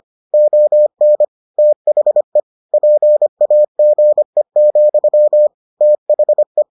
Morse code, heard as a keyed tone, about 25 wpm, 600 Hz. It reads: ON THE PAGE, THE